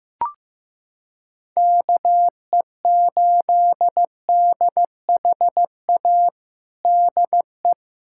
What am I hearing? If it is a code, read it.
Morse code, heard as a keyed tone, about 15 wpm, 700 Hz. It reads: KE8DHA DE
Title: USA Callsign DE Callsign - 6 Characters - 15wpm